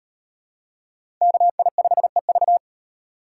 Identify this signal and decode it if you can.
Morse code, heard as a keyed tone, about 35 wpm, 700 Hz. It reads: KI5EV